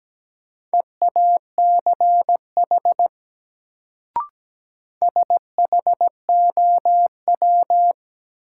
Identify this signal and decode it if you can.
Morse code, heard as a keyed tone, about 17 wpm, 700 Hz. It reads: EACH SHOW
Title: Sentences from Top 200 Words - New Word Drill - 17wpm